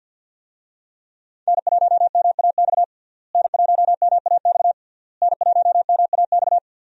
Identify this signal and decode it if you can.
Morse code, heard as a keyed tone, about 48 wpm, 690 Hz. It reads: N1MAX N1MAX N1MAX
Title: USA Call Signs - 5 Characters - Repeated 3x - 50wpm